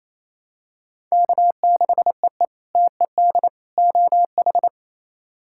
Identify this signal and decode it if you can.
Morse code, heard as a keyed tone, about 28 wpm, 700 Hz. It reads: K6EE TED OH
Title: K1USN SST QSO Practice - 28wpm